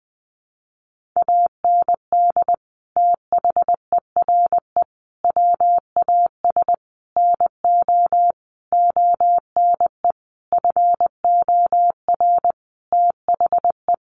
Morse code, heard as a keyed tone, about 19 wpm, 700 Hz. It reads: AND THERE WAS NO ONE FOR THE
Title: Sentences from Top 1500 Words 20wpm